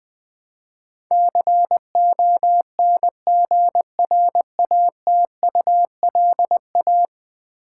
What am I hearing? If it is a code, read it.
Morse code, heard as a keyed tone, about 20 wpm, 700 Hz. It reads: CONGRATULA
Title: Build A Word - 5 Letters At A Time - 12-Letter Words - 20wpm